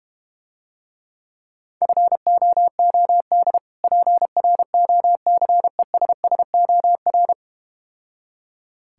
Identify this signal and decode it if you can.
Morse code, heard as a keyed tone, about 32 wpm, 700 Hz. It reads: FOOD PROCESSOR